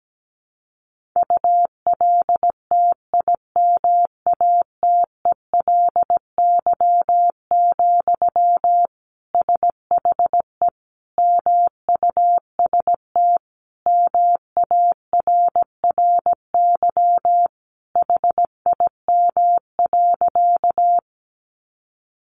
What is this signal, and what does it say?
Morse code, heard as a keyed tone, about 17 wpm, 700 Hz. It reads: ULTIMATELY, SHE MUST MARRY HIM.
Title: Sentences from Top 1200 Words 17wpm